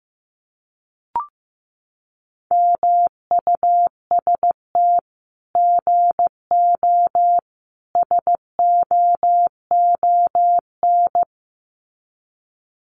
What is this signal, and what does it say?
Morse code, heard as a keyed tone, about 15 wpm, 700 Hz. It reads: MUST GO SOON
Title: Common QSO Phrases 15wpm